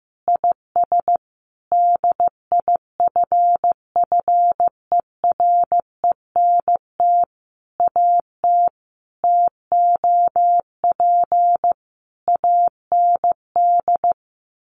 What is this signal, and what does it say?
Morse code, heard as a keyed tone, about 14 wpm, 700 Hz. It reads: IS DIFFERENT AT TOP AND
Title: Sentences from Top 1400 Words 15wpm